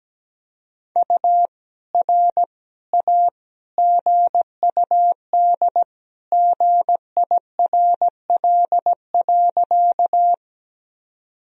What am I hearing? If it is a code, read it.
Morse code, heard as a keyed tone, about 17 wpm, 700 Hz. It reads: U R A GUD GIRL.
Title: Sentences from Top 400 Words and Abbreviations 17wpm